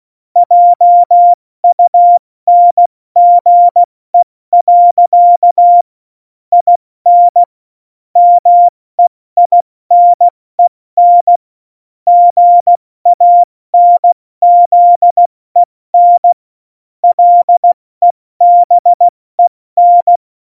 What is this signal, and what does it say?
Morse code, heard as a keyed tone, about 16 wpm, 700 Hz. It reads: JUNGE. IN MEINEN GANZEN LEBEN
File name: Morsekurs / Witze / Witz 05. Dezember 2017, 70 F80